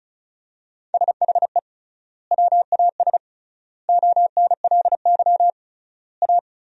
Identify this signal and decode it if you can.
Morse code, heard as a keyed tone, about 34 wpm, 700 Hz. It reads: SHE WAS ONLY A
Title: 3x Word-Spacing - Sentences from Top 400 Words - 35wpm